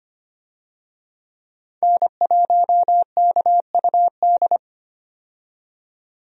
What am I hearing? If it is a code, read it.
Morse code, heard as a keyed tone, about 25 wpm, 700 Hz. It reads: N1KUD
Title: USA Call Signs - 5 Characters - 25wpm